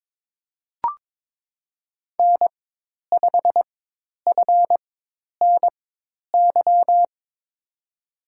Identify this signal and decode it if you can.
Morse code, heard as a keyed tone, about 22 wpm, spaced as for 10 wpm, 700 Hz. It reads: N5FNY
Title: USA Call Signs - 5 Characters - 4x Character Spacing - 22wpm